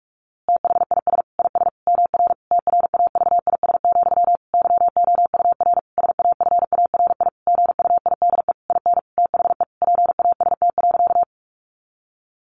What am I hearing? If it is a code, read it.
Morse code, heard as a keyed tone, about 44 wpm, 700 Hz. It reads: THIS IS MR TRAVIS, YOUR SAFARI GUIDE IN THE PAST.
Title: A Sound Of Thunder 45wpm